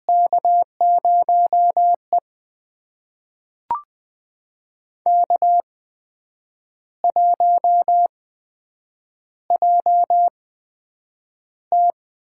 Morse code, heard as a keyed tone, about 20 wpm, spaced as for 8 wpm, 700 Hz. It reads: K0E K1JT
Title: USA Call Signs - 4 Characters - 8x Character Spacing - 20wpm